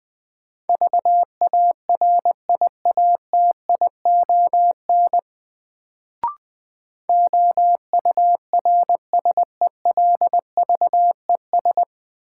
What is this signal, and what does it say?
Morse code, heard as a keyed tone, about 21 wpm, 700 Hz. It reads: VARIATION OURSELVES
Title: Sentences from Top 1600 Words - New Word Drill - 20wpm